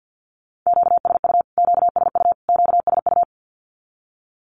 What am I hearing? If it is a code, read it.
Morse code, heard as a keyed tone, about 50 wpm, 700 Hz. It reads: QSU QSU QSU
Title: Q-Codes Repeated 3x 50wpm